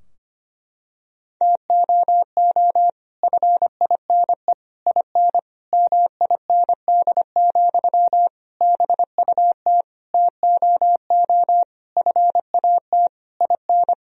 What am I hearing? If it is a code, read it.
Morse code, heard as a keyed tone, about 25 wpm, 700 Hz. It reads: TOO FINE IN MIND, BUT TOO FAT IN
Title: Sentences from Top 1000 Words - Review Part 4 - 25wpm